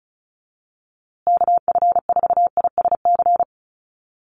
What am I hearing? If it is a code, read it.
Morse code, heard as a keyed tone, about 34 wpm, 700 Hz. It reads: KF4ISC
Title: USA Call Signs Repeated 3x 35wpm